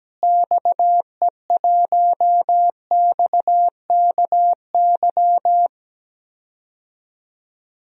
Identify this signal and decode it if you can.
Morse code, heard as a keyed tone, about 17 wpm, 700 Hz. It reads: XE1XKY
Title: DXCC-100 Call Signs - 6 Characters - Repeated 3x - 17wpm